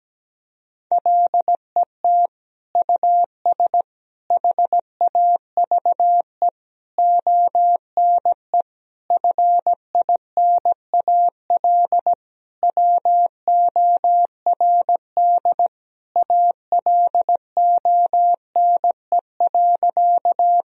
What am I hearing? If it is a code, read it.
Morse code, heard as a keyed tone, about 17 wpm, 700 Hz. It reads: LET US HAVE ONE FINAL WORD ALONE.
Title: Sentences from Top 1000 Words - Review Part 2 - 17wpm